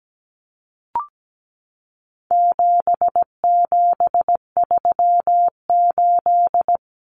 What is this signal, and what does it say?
Morse code, heard as a keyed tone, about 17 wpm, 700 Hz. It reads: 7738
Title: Numbers - 4-Digit 17wpm